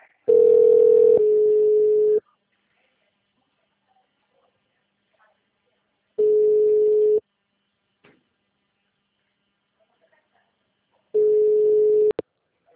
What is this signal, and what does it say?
Ringback tone (Europe, 425 Hz).